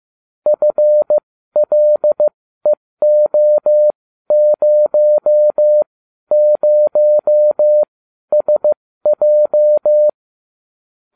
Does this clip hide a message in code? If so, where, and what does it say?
Yes: Morse code from 0.5 to 10.1 s, heard as a keyed tone, about 15 wpm, spaced as for 12 wpm, 600 Hz. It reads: FLEO00SJ